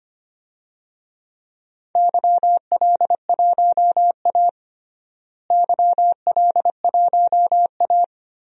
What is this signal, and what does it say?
Morse code, heard as a keyed tone, about 25 wpm, 700 Hz. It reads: YL1A YL1A
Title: DXCC-100 Call Signs - 4 Characters - Repeated 3x - 25wpm